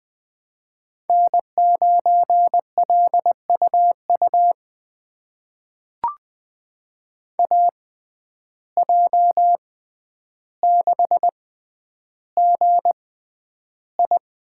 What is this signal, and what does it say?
Morse code, heard as a keyed tone, about 20 wpm, spaced as for 10 wpm, 700 Hz. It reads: N9LUU AJ6GI
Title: USA Call Signs - 5 Characters - 6x Character Spacing - 20wpm